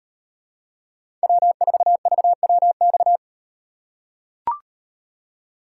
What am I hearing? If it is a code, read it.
Morse code, heard as a keyed tone, about 38 wpm, 700 Hz. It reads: W4VWX